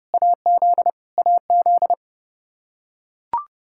Morse code, heard as a keyed tone, about 29 wpm, 700 Hz. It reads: AZ AZ
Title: Practice QSO - Exchange 1 - Beginning - 30wpm